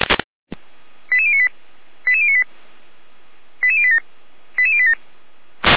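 Five-tone selective call (ZVEI), heard as quick runs of five tones, tones of about 70 ms. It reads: 88098 88098 88087 88087
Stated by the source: ZVEI five-tone selective call received on SDR.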